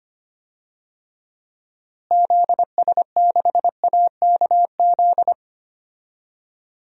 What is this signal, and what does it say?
Morse code, heard as a keyed tone, about 25 wpm, 700 Hz. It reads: ZS6AKZ
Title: DXCC-100 Call Signs - 6 Characters - Repeated 3x - 25wpm